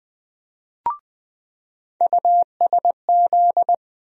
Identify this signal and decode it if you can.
Morse code, heard as a keyed tone, about 20 wpm, 700 Hz. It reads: USZ
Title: USA Call Sign Postfixes - 3 Letters 20wpm